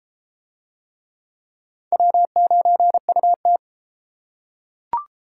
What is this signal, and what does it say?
Morse code, heard as a keyed tone, about 32 wpm, 700 Hz. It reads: W9UT